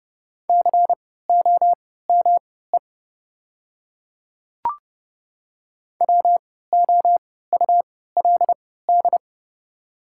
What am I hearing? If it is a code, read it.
Morse code, heard as a keyed tone, about 30 wpm, spaced as for 17 wpm, 700 Hz. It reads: COME WOULD
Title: 3x Character Spacing - Top 100 Words - 30wpm